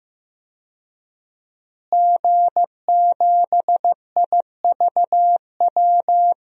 Morse code, heard as a keyed tone, about 15 wpm, 700 Hz. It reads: G7IVW
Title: DXCC-100 Call Signs - 5 Characters - Repeated 3x - 15wpm